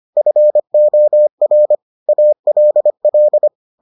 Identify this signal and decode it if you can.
Morse code, heard as a keyed tone, about 25 wpm, 600 Hz. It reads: FOR ALL